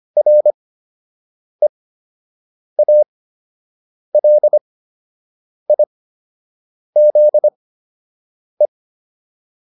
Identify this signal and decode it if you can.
Morse code, heard as a keyed tone, about 25 wpm, spaced as for 7 wpm, 600 Hz. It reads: REALIZE